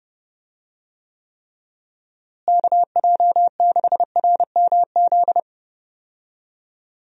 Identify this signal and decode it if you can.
Morse code, heard as a keyed tone, about 30 wpm, 700 Hz. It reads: KJ6RMZ
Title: USA Call Signs - 6 Characters - Repeated 3x - 30wpm